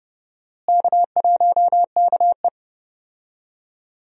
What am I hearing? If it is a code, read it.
Morse code, heard as a keyed tone, about 30 wpm, 700 Hz. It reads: K1KE
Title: USA Call Signs - 4 Characters - Repeated 3x - 30wpm